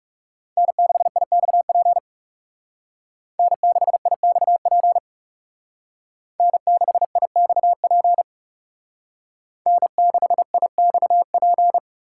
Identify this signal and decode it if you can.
Morse code, heard as a keyed tone, about 34 wpm, 700 Hz. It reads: N6IXP N6IXP N6IXP N6IXP
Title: Speed Racing - USA 1x3 Call Signs 30wpm